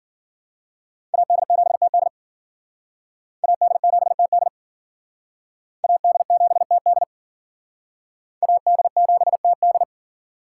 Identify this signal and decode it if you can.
Morse code, heard as a keyed tone, about 50 wpm, 700 Hz. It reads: AD7TD AD7TD AD7TD AD7TD
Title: Speed Racing - USA 2x2 Call Signs 40wpm